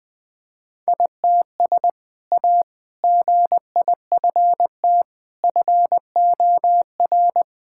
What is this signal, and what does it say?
Morse code, heard as a keyed tone, about 20 wpm, 700 Hz. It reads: ITS A GIFT FOR
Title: Sentences with 2-Word Phrases - 20wpm